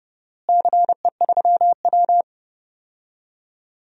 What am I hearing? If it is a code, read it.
Morse code, heard as a keyed tone, about 32 wpm, 700 Hz. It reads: CE3W
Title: DXCC-100 Call Signs - 4 Characters - Repeated 3x - 30wpm